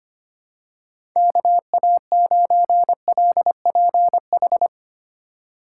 Morse code, heard as a keyed tone, about 25 wpm, 700 Hz. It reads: KA9LPH
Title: USA Call Signs - 6 Characters - 2x Character Spacing - 25wpm